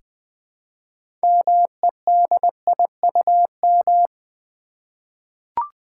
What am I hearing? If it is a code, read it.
Morse code, heard as a keyed tone, about 20 wpm, 700 Hz. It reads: MEDIUM
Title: Six Letter Words - 20wpm